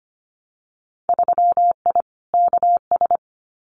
Morse code, heard as a keyed tone, about 25 wpm, 700 Hz. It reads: 3I KS
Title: Winter Field Day Exchanges 25wpm